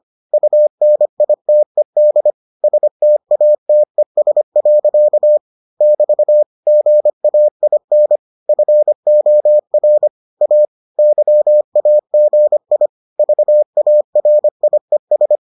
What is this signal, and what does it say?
Morse code, heard as a keyed tone, about 25 wpm, 600 Hz. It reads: UNITED STATES. <BT> GAIN FOR A YAGI VARIES